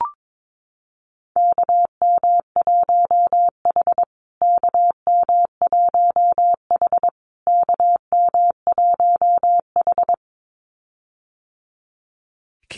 Morse code, heard as a keyed tone, about 22 wpm, 700 Hz. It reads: KM1H KM1H KM1H
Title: USA Call Signs Repeated 3x 22wpm